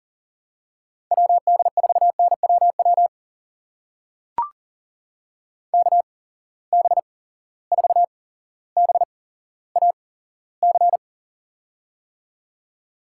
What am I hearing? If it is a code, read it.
Morse code, heard as a keyed tone, about 39 wpm, spaced as for 17 wpm, 700 Hz. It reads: WD4NWW KB4BAC
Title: USA Call Signs - 6 Characters - 8x Character Spacing - 40wpm